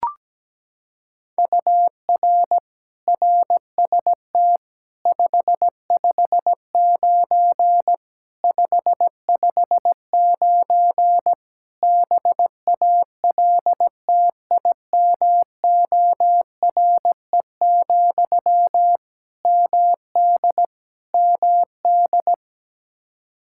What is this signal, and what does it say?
Morse code, heard as a keyed tone, about 17 wpm, 700 Hz. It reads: UR RST 559 559 BALTIMORE, MD MD
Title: Practice QSO - Exchange 1 - Beginning - 17wpm